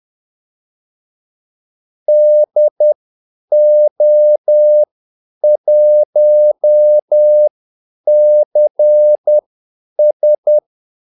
Morse code, heard as a keyed tone, about 10 wpm, spaced as for 8 wpm, 600 Hz. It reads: DO1CS
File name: Callsigns-10-8wpm